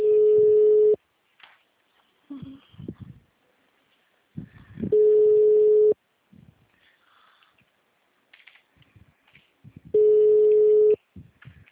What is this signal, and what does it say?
Ringback tone (Europe, 425 Hz).